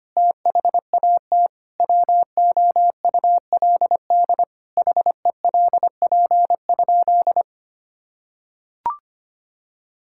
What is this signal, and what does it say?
Morse code, heard as a keyed tone, about 25 wpm, 700 Hz. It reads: THAT WOULD HELP?
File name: practice / 25wpm / Sentences with 9-Word Phrases - 25wpm